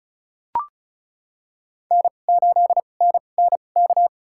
Morse code, heard as a keyed tone, about 36 wpm, spaced as for 25 wpm, 700 Hz. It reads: N8NNK
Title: USA Call Signs - 2x Character Spacing - 35wpm